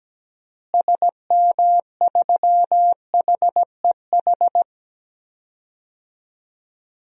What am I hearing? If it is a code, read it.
Morse code, heard as a keyed tone, about 17 wpm, 700 Hz. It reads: SM3HEH
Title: DXCC-100 Call Signs - 6 Characters - Repeated 2x - 17wpm